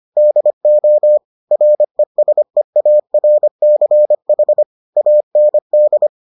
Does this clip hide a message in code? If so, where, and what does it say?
Yes: Morse code from 0.2 to 6.1 s, heard as a keyed tone, about 25 wpm, 600 Hz. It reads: DO RESEARCH AND